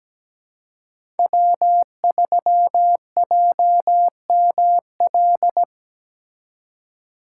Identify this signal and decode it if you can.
Morse code, heard as a keyed tone, about 20 wpm, 700 Hz. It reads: W3JML
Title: Speed Racing - USA 1x3 Call Signs 17wpm